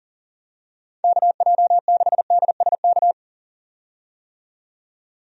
Morse code, heard as a keyed tone, about 40 wpm, 700 Hz. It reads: KJ6DSK
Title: USA Call Signs - 6 Characters - 40wpm